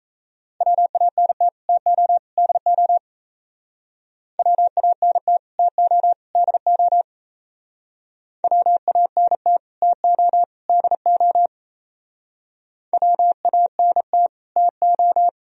Morse code, heard as a keyed tone, about 37 wpm, 700 Hz. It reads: WANT TO DO WANT TO DO WANT TO DO WANT TO